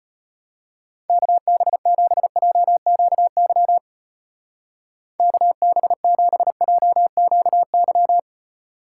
Morse code, heard as a keyed tone, about 32 wpm, 700 Hz. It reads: KB7JQY KB7JQY